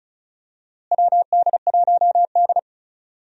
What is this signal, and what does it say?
Morse code, heard as a keyed tone, about 34 wpm, 700 Hz. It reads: WD1D
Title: USA Call Signs - 4 Characters - Repeated 2x - 35wpm